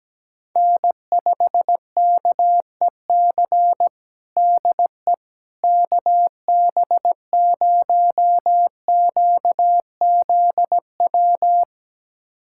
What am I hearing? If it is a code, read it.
Morse code, heard as a keyed tone, about 17 wpm, 700 Hz. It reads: N5KEC DE KB0QZW
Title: USA Callsign DE Callsign 17wpm